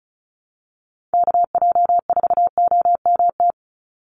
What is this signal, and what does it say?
Morse code, heard as a keyed tone, about 34 wpm, 700 Hz. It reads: KJ4OMT
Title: USA Call Signs 35wpm